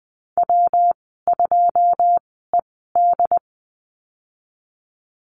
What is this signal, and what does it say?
Morse code, heard as a keyed tone, about 20 wpm, spaced as for 15 wpm, 700 Hz. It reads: W2ED
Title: 122 Lesson 20g - Number 3 - Call Signs 20wpm